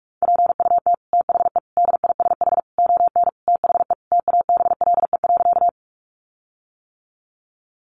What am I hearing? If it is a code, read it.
Morse code, heard as a keyed tone, about 44 wpm, 700 Hz. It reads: PUT THE DISH ON THE TABLE.
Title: Sentences from Top 1300 Words 45wpm